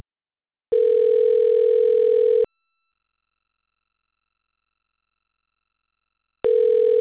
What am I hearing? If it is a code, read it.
Ringback tone (North America), 440 + 480 Hz.